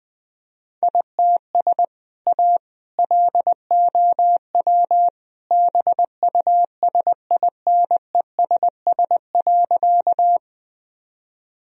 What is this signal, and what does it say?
Morse code, heard as a keyed tone, about 20 wpm, 700 Hz. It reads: ITS A LOW BUSINESS.
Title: Sentences from Top 1000 Words - Review Part 1 - 20wpm